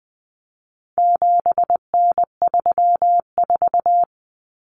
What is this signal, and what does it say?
Morse code, heard as a keyed tone, about 20 wpm, 700 Hz. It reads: 7N34
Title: Common Cut Numbers - 4-Digit 20wpm